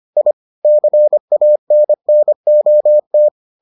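Morse code, heard as a keyed tone, about 25 wpm, 600 Hz. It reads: I CANNOT
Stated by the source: Morse edition of the Foundations of Amateur Radio episodes.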